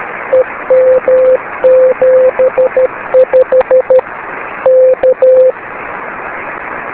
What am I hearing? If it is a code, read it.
Morse code, heard as a keyed tone, about 13 wpm, 520 Hz, with radio noise. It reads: EM75 K
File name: beacons / k4kwk-b